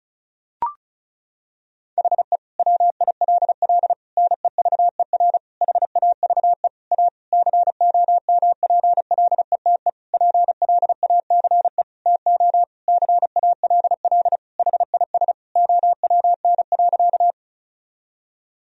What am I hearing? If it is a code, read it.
Morse code, heard as a keyed tone, about 34 wpm, 700 Hz. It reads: HE WILL NEVER HAVE A COMPLETE PLACE TO CALL HIS OWN.
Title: Sentences from Top 1000 Words - Review Part 2 - 35wpm